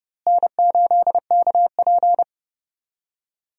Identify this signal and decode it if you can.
Morse code, heard as a keyed tone, about 29 wpm, 700 Hz. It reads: N8KP
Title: USA Call Signs - 4 Characters - Repeated 2x - 30wpm